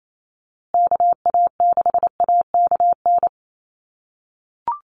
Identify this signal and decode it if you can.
Morse code, heard as a keyed tone, about 28 wpm, 700 Hz. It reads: KA6AKN
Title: USA Call Signs Repeated 3x 28wpm